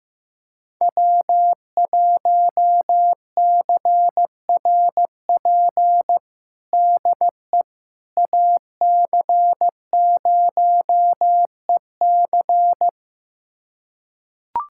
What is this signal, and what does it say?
Morse code, heard as a keyed tone, about 15 wpm, 700 Hz. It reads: W1CRP DE AC0EC
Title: USA Callsign DE Callsign - 5 Characters - 15wpm